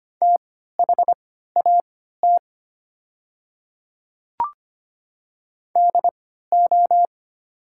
Morse code, heard as a keyed tone, about 25 wpm, spaced as for 14 wpm, 700 Hz. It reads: THAT DO
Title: 3x Character Spacing - Top 100 Words - 25wpm